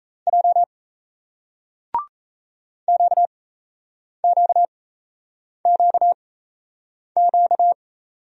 Morse code, heard as a keyed tone, about 37 wpm, spaced as for 8 wpm, 700 Hz. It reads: J QQQQ